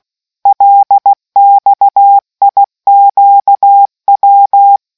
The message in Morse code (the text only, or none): LXIQW